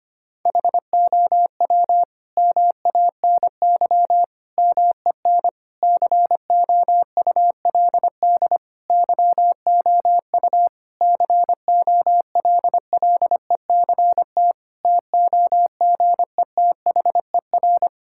Morse code, heard as a keyed tone, about 25 wpm, 700 Hz. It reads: HOW MANY MEN COULD YOU COLLECT TOGETHER
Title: Sentences from Top 1600 Words 25wpm